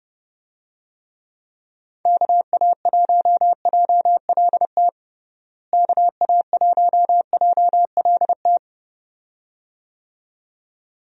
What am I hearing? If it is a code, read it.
Morse code, heard as a keyed tone, about 30 wpm, 700 Hz. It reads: KA1JLT KA1JLT
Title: USA Call Signs - 6 Characters - Repeated 2x - 30wpm